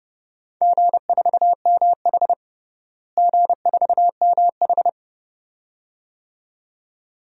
Morse code, heard as a keyed tone, about 31 wpm, 700 Hz. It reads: G4MH G4MH
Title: DXCC-100 Call Signs - 4 Characters - Repeated 2x - 30wpm